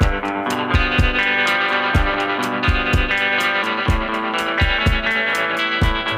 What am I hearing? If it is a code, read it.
Emergency Alert System (SAME) tones (header bursts), heard as harsh buzzing bursts. It is heard from 1.2 to 5.4 s.